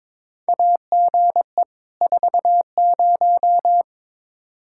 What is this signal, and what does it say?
Morse code, heard as a keyed tone, about 22 wpm, 700 Hz. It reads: AGE 40
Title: Practice QSO - Exchange 3 - More Info - 22wpm